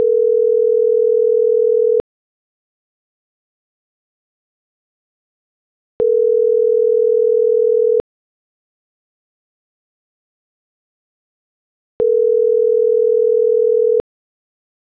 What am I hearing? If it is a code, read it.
Ringback tone (North America), 440 + 480 Hz.